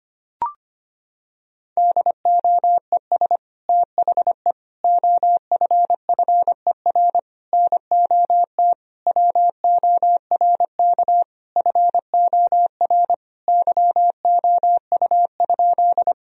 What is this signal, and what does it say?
Morse code, heard as a keyed tone, about 25 wpm, 700 Hz. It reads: DOES THE OFFER NOT WORK FOR YOU?
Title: Sentences from Top 1000 Words - Review Part 4 - 25wpm